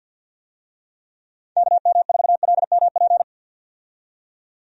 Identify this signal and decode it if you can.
Morse code, heard as a keyed tone, about 48 wpm, 700 Hz. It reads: KM4LMP
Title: USA Call Signs - 6 Characters - 50wpm